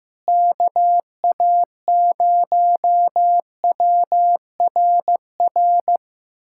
Morse code, heard as a keyed tone, about 15 wpm, 700 Hz. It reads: KA0WRR
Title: USA Callsign DE Callsign 15wpm